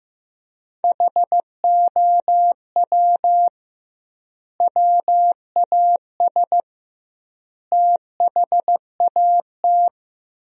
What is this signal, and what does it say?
Morse code, heard as a keyed tone, about 15 wpm, 700 Hz. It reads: HOW WAS THAT
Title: 2x Word-Spacing - Sentences from Top 500 Words - 15wpm